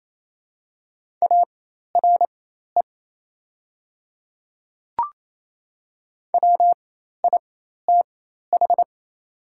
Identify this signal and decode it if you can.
Morse code, heard as a keyed tone, about 28 wpm, spaced as for 13 wpm, 700 Hz. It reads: ARE WITH
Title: 4x Character Spacing - Top 100 Words - 28wpm